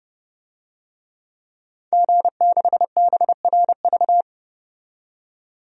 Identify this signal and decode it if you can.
Morse code, heard as a keyed tone, about 31 wpm, 700 Hz. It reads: G6BRV
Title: DXCC-100 Call Signs - 5 Characters - 30wpm